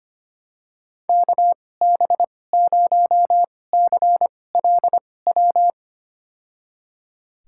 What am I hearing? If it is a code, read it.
Morse code, heard as a keyed tone, about 25 wpm, spaced as for 18 wpm, 700 Hz. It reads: KB0CLW
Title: USA Call Signs - 2x Character Spacing - 25wpm